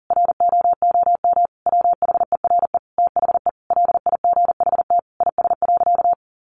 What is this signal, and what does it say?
Morse code, heard as a keyed tone, about 40 wpm, 700 Hz. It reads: ROOM WHERE THE LIGHT IS.
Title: Sentences from Top 1000 Words 40wpm